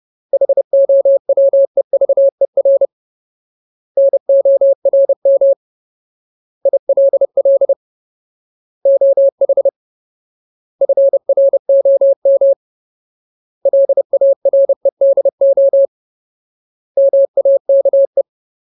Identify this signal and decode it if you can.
Morse code, heard as a keyed tone, about 30 wpm, 550 Hz. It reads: HOWEVER NORM ILL OH FROM LAREDO MAKE
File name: Ultimate Word List / 30WPM Ultimate